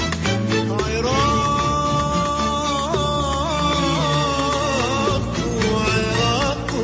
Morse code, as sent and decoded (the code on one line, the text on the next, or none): none